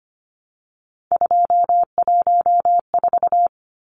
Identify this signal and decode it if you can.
Morse code, heard as a keyed tone, about 25 wpm, 700 Hz. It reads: 214
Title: Numbers - 3-Digit 25wpm